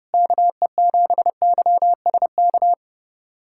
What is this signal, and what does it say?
Morse code, heard as a keyed tone, about 30 wpm, 700 Hz. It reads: KE7YSK
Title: USA Callsign DE Callsign 30wpm